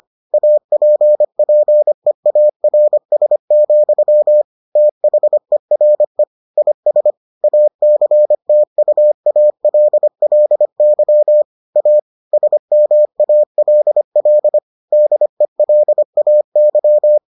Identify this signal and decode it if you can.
Morse code, heard as a keyed tone, about 25 wpm, 600 Hz. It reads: APPEARS, THERE IS ACTUALLY A SMALL DELAY